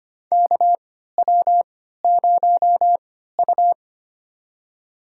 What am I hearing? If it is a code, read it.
Morse code, heard as a keyed tone, about 25 wpm, spaced as for 14 wpm, 700 Hz. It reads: KW0U
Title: USA Call Signs - 3x Character Spacing - 25wpm